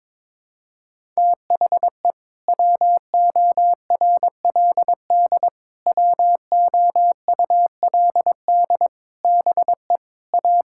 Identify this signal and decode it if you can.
Morse code, heard as a keyed tone, about 22 wpm, 700 Hz. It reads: THE WORLD WOULD BE A
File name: practice / 22wpm / Sentences with 8-Word Phrases - 22wpm